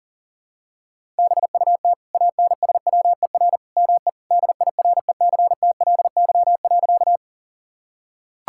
Morse code, heard as a keyed tone, about 40 wpm, 700 Hz. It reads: BUT ANSWER ME DIRECTLY.
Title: Sentences from Top 1000 Words - Review Part 5 - 40wpm